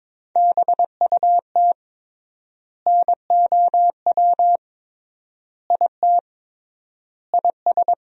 Morse code, heard as a keyed tone, about 22 wpm, 700 Hz. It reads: BUT NOW IT IS
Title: 3x Word-Spacing - Sentences from Top 300 Words - 22wpm